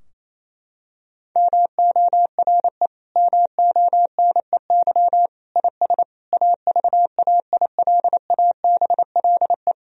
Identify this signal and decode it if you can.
Morse code, heard as a keyed tone, about 28 wpm, 700 Hz. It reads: MORE MONEY IS AVAILABLE
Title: Sentences from Top 1000 Words - Review Part 3 - 28wpm